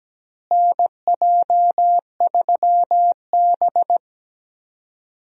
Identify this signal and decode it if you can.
Morse code, heard as a keyed tone, about 17 wpm, 700 Hz. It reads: NJ3B
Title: USA Call Signs - 4 Characters - Repeated 2x - 17wpm